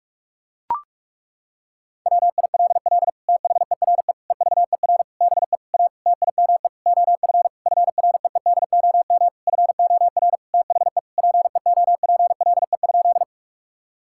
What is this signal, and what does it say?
Morse code, heard as a keyed tone, about 41 wpm, 700 Hz. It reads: WILL THERE EVER BE A TIME OF FREEDOM FOR THE PEOPLE?
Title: Sentences from Top 1000 Words - Review Part 5 - 45wpm